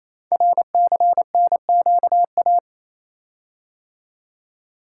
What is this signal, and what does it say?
Morse code, heard as a keyed tone, about 28 wpm, 700 Hz. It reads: RCNQA
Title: Code Groups - 5 Letters - 28wpm